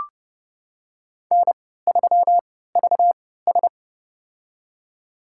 Morse code, heard as a keyed tone, about 29 wpm, spaced as for 17 wpm, 700 Hz. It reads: N3VS
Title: USA Call Signs - 4 Characters - 3x Character Spacing - 30wpm